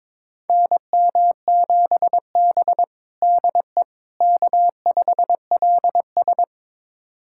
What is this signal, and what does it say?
Morse code, heard as a keyed tone, about 22 wpm, 700 Hz. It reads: NM7B DE K5LS
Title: USA Callsign DE Callsign - 4 Characters - 22wpm